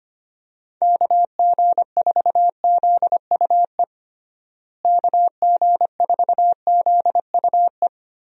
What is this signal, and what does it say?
Morse code, heard as a keyed tone, about 25 wpm, 700 Hz. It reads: KG4ZUE KG4ZUE
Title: USA Call Signs - 6 Characters - Repeated 2x - 25wpm